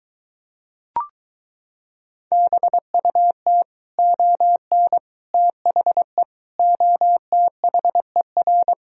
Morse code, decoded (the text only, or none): BUT ON THE OTHER